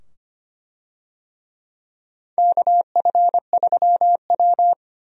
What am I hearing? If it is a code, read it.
Morse code, heard as a keyed tone, about 25 wpm, 700 Hz. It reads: KF3W
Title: USA Call Signs - 4 Characters - 25wpm